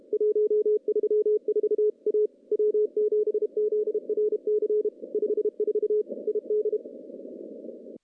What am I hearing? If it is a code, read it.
Morse code, heard as a keyed tone, about 32 wpm, 410 Hz. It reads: 134 A W7ZRC 54 ID
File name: contests / CW-SS-k5tr- -0000